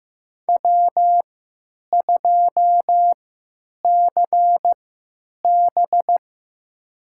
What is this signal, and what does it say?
Morse code, heard as a keyed tone, about 15 wpm, spaced as for 9 wpm, 700 Hz. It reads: W2CB
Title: USA Call Signs - 3x Character Spacing - 15wpm